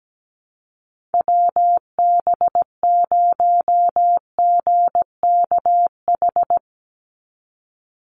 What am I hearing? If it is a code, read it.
Morse code, heard as a keyed tone, about 17 wpm, 700 Hz. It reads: WB0GKH
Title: Call Signs Repeated 3x - 17wpm